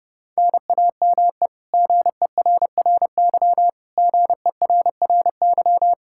Morse code, heard as a keyed tone, about 30 wpm, 700 Hz. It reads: NAME GERRY GERRY
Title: Practice QSO - Exchange 1 - Beginning - 30wpm